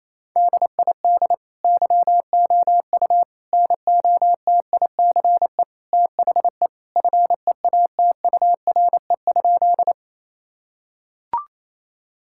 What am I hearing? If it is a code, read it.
Morse code, heard as a keyed tone, about 28 wpm, 700 Hz. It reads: DID YOU NOTICE THE FEATURE?
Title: Sentences from Top 1000 Words - Review Part 5 - 28wpm